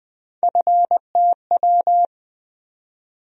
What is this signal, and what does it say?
Morse code, heard as a keyed tone, about 20 wpm, 700 Hz. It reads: FTW
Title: USA Call Sign Postfixes - 3 Letters 20wpm